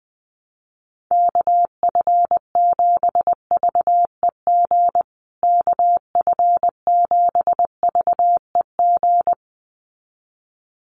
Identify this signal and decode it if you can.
Morse code, heard as a keyed tone, about 20 wpm, 700 Hz. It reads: KF7VEG KF7VEG
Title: USA Call Signs Repeated 2x 20wpm